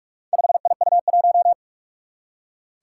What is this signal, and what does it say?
Morse code, heard as a keyed tone, about 44 wpm, 700 Hz. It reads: 5IU1
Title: Four Challenging Letter-Number Combinations 45wpm